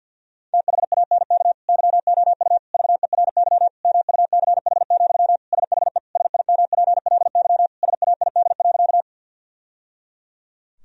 Morse code, heard as a keyed tone, about 50 wpm, 700 Hz. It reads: THANK YOU VERY MUCH, SHE SIMPLY SAID.
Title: Sentences from Top 1000 Words - Review Part 4 - 50wpm